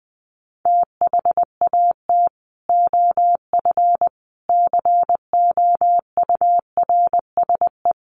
Morse code, heard as a keyed tone, about 19 wpm, 700 Hz. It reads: THAT OF COURSE
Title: Sentences from Top 1500 Words 20wpm